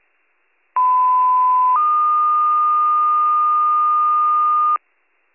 Two-tone page, heard as a one-second tone, then a three-second tone, about 1010 then 1220 Hz, with noise. It is heard from 0.8 to 4.8 s.